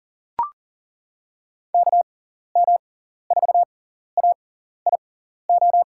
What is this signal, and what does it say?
Morse code, heard as a keyed tone, about 40 wpm, spaced as for 14 wpm, 700 Hz. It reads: KM4AIO
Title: USA Call Signs - 6 Characters - 6x Character Spacing - 40wpm